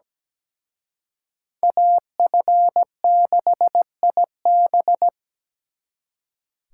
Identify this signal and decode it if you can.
Morse code, heard as a keyed tone, about 20 wpm, 700 Hz. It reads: AF6IB
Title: Speed Racing - USA 2x2 Call Signs 17wpm